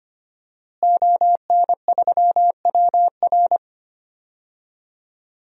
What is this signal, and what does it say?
Morse code, heard as a keyed tone, about 25 wpm, 700 Hz. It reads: ON3WR
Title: DXCC-100 Call Signs - 5 Characters - Repeated 3x - 25wpm